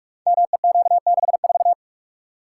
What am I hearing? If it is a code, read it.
Morse code, heard as a keyed tone, about 44 wpm, 700 Hz. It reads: MEQ64
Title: Code Groups - 5 Letter:Numbers - 45wpm